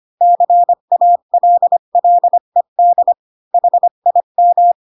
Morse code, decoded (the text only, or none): CALLED HIM